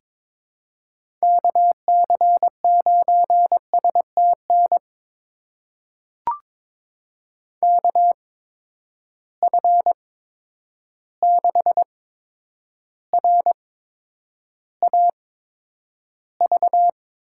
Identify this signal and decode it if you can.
Morse code, heard as a keyed tone, about 21 wpm, spaced as for 9 wpm, 700 Hz. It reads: KC9STN KF6RAV
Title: USA Call Signs - 6 Characters - 8x Character Spacing - 22wpm